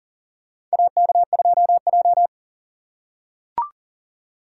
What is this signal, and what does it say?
Morse code, heard as a keyed tone, about 38 wpm, 700 Hz. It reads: AK2J
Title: USA Call Signs - 3x Character Spacing - 40wpm